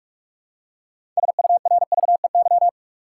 Morse code, heard as a keyed tone, about 46 wpm, 700 Hz. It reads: SURVEY
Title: Six Letter Words - 45wpm